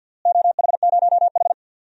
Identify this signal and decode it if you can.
Morse code, heard as a keyed tone, about 51 wpm, 700 Hz. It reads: OH0H
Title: DXCC-100 Call Signs - 4 Characters - Repeated 2x - 50wpm